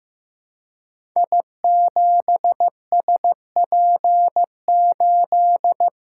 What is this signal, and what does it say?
Morse code, heard as a keyed tone, about 15 wpm, 700 Hz. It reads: I7SP8
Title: Code Groups - 5 Letter:Numbers - 15wpm